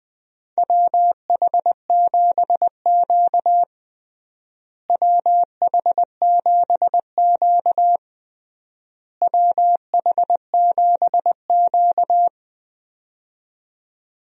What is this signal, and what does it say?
Morse code, heard as a keyed tone, about 20 wpm, 700 Hz. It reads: WH7Q WH7Q WH7Q
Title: USA Call Signs - 4 Characters - Repeated 3x - 20wpm